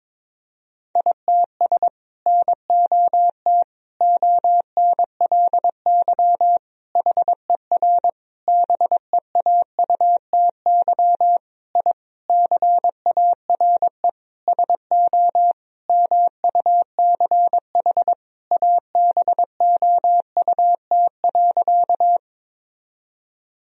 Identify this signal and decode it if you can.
Morse code, heard as a keyed tone, about 22 wpm, 700 Hz. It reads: ITS NOT ONLY HER BEAUTY I CARE SO MUCH ABOUT.
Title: Sentences from Top 1000 Words - Review Part 2 - 22wpm